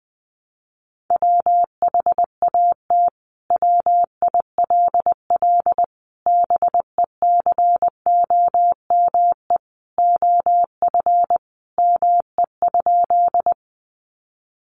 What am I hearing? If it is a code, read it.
Morse code, heard as a keyed tone, about 20 wpm, 700 Hz. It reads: WHAT WILL BECOME OF ME?
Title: Sentences from Top 600 Words 20wpm